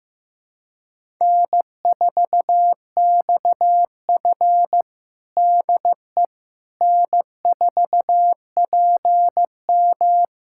Morse code, heard as a keyed tone, about 15 wpm, 700 Hz. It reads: N4XF DE N4PM
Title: USA Callsign DE Callsign - 4 Characters - 15wpm